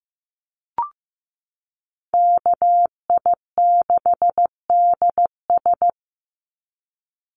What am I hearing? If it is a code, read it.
Morse code, heard as a keyed tone, about 15 wpm, 700 Hz. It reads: KI6DS
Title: NA SOTA Chaser Callsigns 15wpm